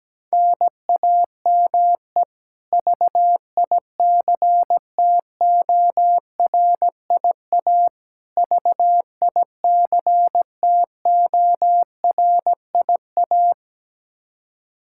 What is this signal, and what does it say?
Morse code, heard as a keyed tone, about 17 wpm, 700 Hz. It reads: NAME VICTORIA VICTORIA
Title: K1USN SST QSO Practice 17wpm